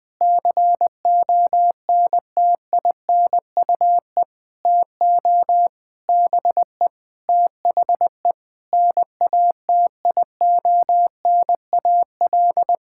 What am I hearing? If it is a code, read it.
Morse code, heard as a keyed tone, about 20 wpm, 700 Hz. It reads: CONTINUE TO BE THE NATIONAL
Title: Sentences from Top 1000 Words - Review Part 6 - 20wpm